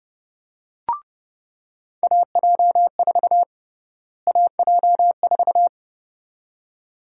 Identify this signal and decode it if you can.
Morse code, heard as a keyed tone, about 32 wpm, 700 Hz. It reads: AJ4 AJ4
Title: USA Call Sign Prefixes - Repeated 2x - 30wpm